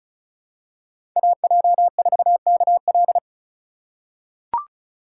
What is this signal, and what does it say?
Morse code, heard as a keyed tone, about 34 wpm, 700 Hz. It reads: AJ4KL
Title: USA Call Signs - 5 Characters - 2x Character Spacing - 35wpm